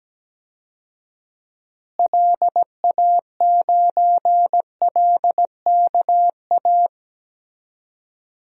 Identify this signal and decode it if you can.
Morse code, heard as a keyed tone, about 17 wpm, 700 Hz. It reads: LA9LKA
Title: DXCC-100 Call Signs - 6 Characters - Repeated 2x - 17wpm